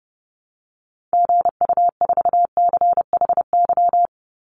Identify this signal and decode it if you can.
Morse code, heard as a keyed tone, about 30 wpm, 700 Hz. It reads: GU4CHY
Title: Call Signs Repeated 3x - 30wpm